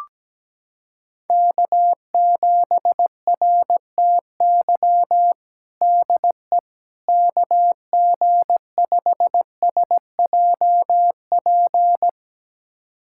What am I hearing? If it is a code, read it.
Morse code, heard as a keyed tone, about 17 wpm, 700 Hz. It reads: K7RTY DE KG5SJP
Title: USA Callsign DE Callsign 17wpm